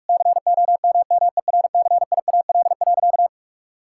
Morse code, heard as a keyed tone, about 45 wpm, 700 Hz. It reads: COMMERCIAL.
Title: Sentences from Top 1000 Words - Review Part 5 - 45wpm